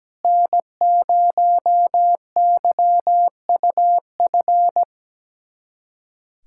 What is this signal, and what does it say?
Morse code, heard as a keyed tone, about 17 wpm, 700 Hz. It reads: N0YUF
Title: USA Callsign DE Callsign - 5 Characters - 17wpm